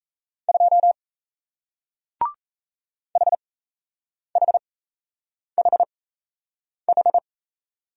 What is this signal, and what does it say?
Morse code, heard as a keyed tone, about 36 wpm, spaced as for 8 wpm, 700 Hz. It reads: 2 HHHH